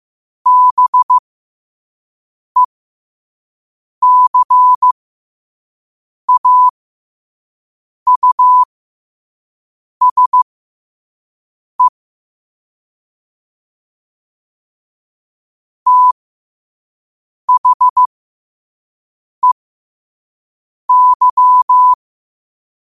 Morse code, heard as a keyed tone, about 15 wpm, spaced as for 5 wpm, 1000 Hz. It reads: BECAUSE THEY
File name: qotdfiles / 5wpm